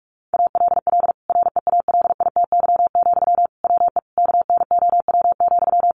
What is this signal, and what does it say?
Morse code, heard as a keyed tone, about 44 wpm, 700 Hz. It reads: ALL REALITY, WE KNOW,
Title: Sentences from Top 1400 Words 45wpm